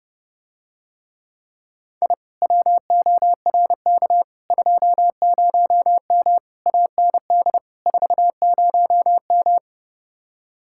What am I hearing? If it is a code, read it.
Morse code, heard as a keyed tone, about 30 wpm, 700 Hz. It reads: I WORK 20M AND 40M